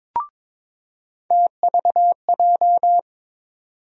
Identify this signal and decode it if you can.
Morse code, heard as a keyed tone, about 22 wpm, 700 Hz. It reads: TVJ
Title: USA Call Sign Postfixes - 3 Letters 22wpm